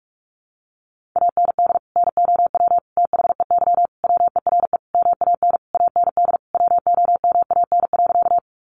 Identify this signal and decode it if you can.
Morse code, heard as a keyed tone, about 44 wpm, 700 Hz. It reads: AND NOW THEY WERE MAN AND WOMAN.
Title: Sentences from Top 1100 Words 45wpm